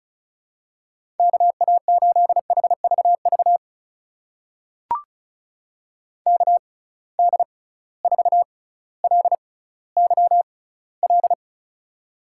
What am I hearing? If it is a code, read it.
Morse code, heard as a keyed tone, about 36 wpm, spaced as for 18 wpm, 700 Hz. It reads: KA8HVV KD4LYL
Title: USA Call Signs - 6 Characters - 6x Character Spacing - 35wpm